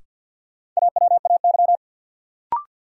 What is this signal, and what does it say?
Morse code, heard as a keyed tone, about 51 wpm, 700 Hz. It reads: AWAY
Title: Sets of 2 Words - New Word Drill - 50wpm